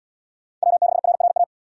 Morse code, heard as a keyed tone, about 69 wpm, 700 Hz. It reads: W6RRA